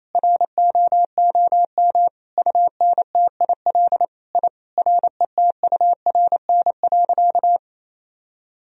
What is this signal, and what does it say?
Morse code, heard as a keyed tone, about 28 wpm, 700 Hz. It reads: ROOM UNTIL I RETURN.
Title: Sentences from Top 1000 Words - Review Part 1 - 28wpm